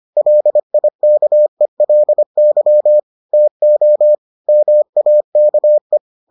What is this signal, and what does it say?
Morse code, heard as a keyed tone, about 25 wpm, 600 Hz. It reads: LIKELY TO MAKE